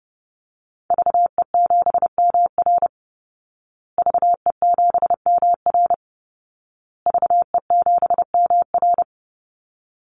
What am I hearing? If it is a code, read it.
Morse code, heard as a keyed tone, about 30 wpm, 700 Hz. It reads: VE7MR VE7MR VE7MR
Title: Call Signs Repeated 2x - 30wpm